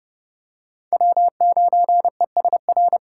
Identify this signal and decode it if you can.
Morse code, heard as a keyed tone, about 30 wpm, 700 Hz. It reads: W9ESR
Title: USA Call Signs - 5 Characters - 2x Character Spacing - 30wpm